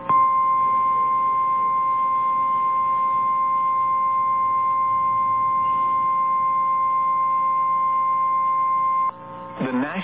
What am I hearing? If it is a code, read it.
Emergency Alert System (SAME) tones (the 1050 Hz attention tone). It is heard from 1.1 to 8.9 s.
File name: eas / KNCYKBIE / KNCYKBIE